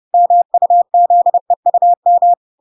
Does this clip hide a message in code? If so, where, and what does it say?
Yes: Morse code from 0.1 to 2.3 s, heard as a keyed tone, about 30 wpm, 700 Hz. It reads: MUZEUM